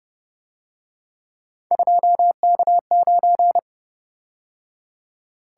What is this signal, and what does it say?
Morse code, heard as a keyed tone, about 30 wpm, 700 Hz. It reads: 2K9